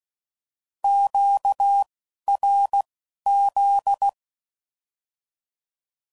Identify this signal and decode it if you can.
Morse code, heard as a keyed tone, about 16 wpm, 800 Hz. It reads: Q R Z